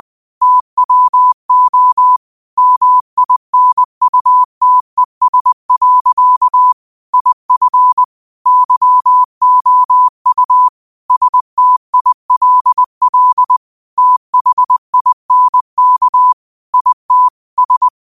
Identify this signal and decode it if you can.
Morse code, heard as a keyed tone, about 20 wpm, 1000 Hz. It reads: TWO MINUTES. IF YOU STILL THINK IT S